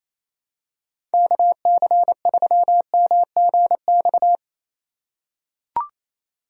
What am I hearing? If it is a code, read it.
Morse code, heard as a keyed tone, about 28 wpm, 700 Hz. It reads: KC3MGX
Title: USA Call Signs - 6 Characters - 2x Character Spacing - 28wpm